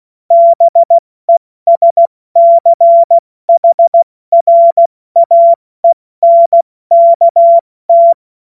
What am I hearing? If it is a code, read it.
Morse code, heard as a keyed tone, about 16 wpm, spaced as for 14 wpm, 680 Hz. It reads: BESCHRAENKT
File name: Morsekurs / Witze / Witz 21. Juni 2022 70 F80